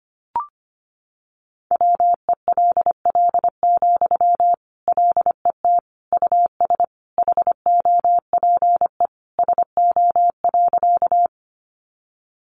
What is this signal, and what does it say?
Morse code, heard as a keyed tone, about 25 wpm, 700 Hz. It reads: WELL, LET US HOPE SO.
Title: Sentences from Top 1000 Words 25wpm